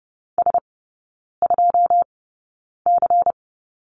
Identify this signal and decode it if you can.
Morse code, heard as a keyed tone, about 30 wpm, spaced as for 9 wpm, 700 Hz. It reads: S2C
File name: Lesson 18e - Letter m - Three Characters 30wpm